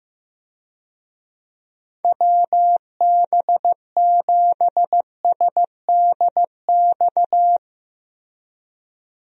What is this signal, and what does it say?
Morse code, heard as a keyed tone, about 15 wpm, 700 Hz. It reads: WB7SDX
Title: USA Call Signs - 6 Characters - Repeated 3x - 15wpm